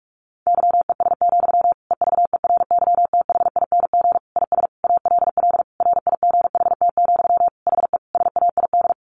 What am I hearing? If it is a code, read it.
Morse code, heard as a keyed tone, about 44 wpm, 700 Hz. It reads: YES, EVERYTHING IS ALL RIGHT, HE SAID
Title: Sentences from Top 1100 Words 45wpm